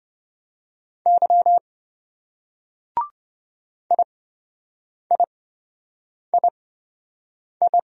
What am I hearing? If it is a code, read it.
Morse code, heard as a keyed tone, about 27 wpm, spaced as for 7 wpm, 700 Hz. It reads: Y IIII